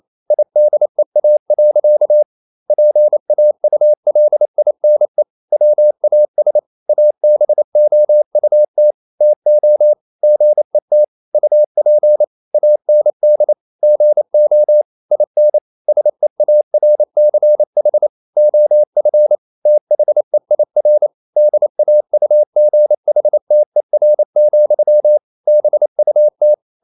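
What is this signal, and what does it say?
Morse code, heard as a keyed tone, about 28 wpm, 600 Hz. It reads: IDEA. PAULINE WAS ABOUT TO GET UP AND GO IN SEARCH OF THEIR DAUGHTER, BUT
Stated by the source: Morse code practice at 28 wpm, text of Ken Follett's novel Never in English, from CW med Gnister.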